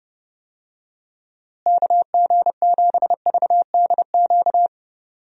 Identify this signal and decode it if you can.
Morse code, heard as a keyed tone, about 30 wpm, 700 Hz. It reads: KG7VDQ
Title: USA Call Signs - 6 Characters - Repeated 2x - 30wpm